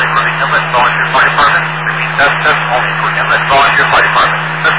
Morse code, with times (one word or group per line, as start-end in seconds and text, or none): none